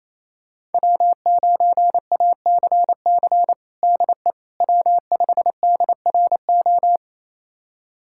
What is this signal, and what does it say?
Morse code, heard as a keyed tone, about 28 wpm, 700 Hz. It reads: W9ACC DE W5DRO
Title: USA Callsign DE Callsign - 5 Characters - 28wpm